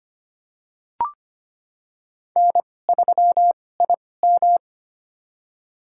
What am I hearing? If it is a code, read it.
Morse code, heard as a keyed tone, about 25 wpm, spaced as for 18 wpm, 700 Hz. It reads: N3IM
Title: USA Call Signs - 4 Characters - 2x Character Spacing - 25wpm